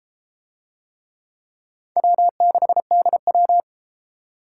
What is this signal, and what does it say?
Morse code, heard as a keyed tone, about 30 wpm, 700 Hz. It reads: W6DW